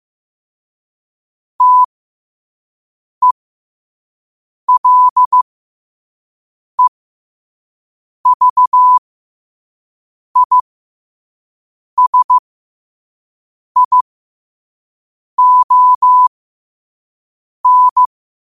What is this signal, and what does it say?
Morse code, heard as a keyed tone, about 15 wpm, spaced as for 5 wpm, 1000 Hz. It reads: TELEVISION
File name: qotdfiles / 5wpm